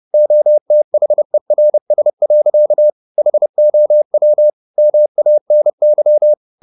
Morse code, heard as a keyed tone, about 30 wpm, 600 Hz. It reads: OTHERS. HOW MANY